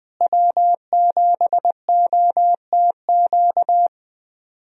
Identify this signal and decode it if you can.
Morse code, heard as a keyed tone, about 20 wpm, 700 Hz. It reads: W7OTQ
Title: USA Call Signs - 5 Characters - Repeated 2x - 20wpm